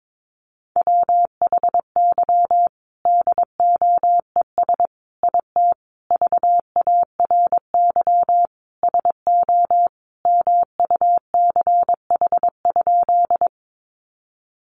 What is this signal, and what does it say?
Morse code, heard as a keyed tone, about 22 wpm, 700 Hz. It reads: WHY DOES IT VARY SO MUCH?
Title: Sentences from Top 1200 Words 22wpm